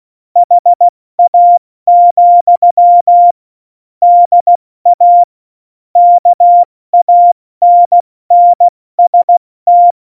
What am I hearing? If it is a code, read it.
Morse code, heard as a keyed tone, about 16 wpm, 700 Hz. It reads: HA, DA KANNST
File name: Morsekurs / Witze / Witz 03. Oktober 17, Tempo 70 F80